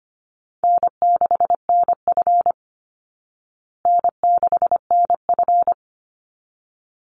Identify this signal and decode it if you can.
Morse code, heard as a keyed tone, about 25 wpm, 700 Hz. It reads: N6NF N6NF
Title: Easy Call Signs Repeated 2x - 25wpm